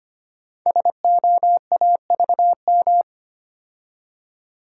Morse code, heard as a keyed tone, about 25 wpm, 700 Hz. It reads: SOAVM
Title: Code Groups - 5 Letters - 25wpm